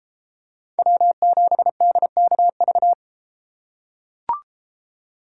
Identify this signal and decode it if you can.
Morse code, heard as a keyed tone, about 32 wpm, 700 Hz. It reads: W7DKV